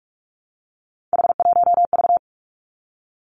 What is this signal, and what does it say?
Morse code, heard as a keyed tone, about 46 wpm, 700 Hz. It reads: H1V
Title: Three Challenging Letter-Number Combinations 45wpm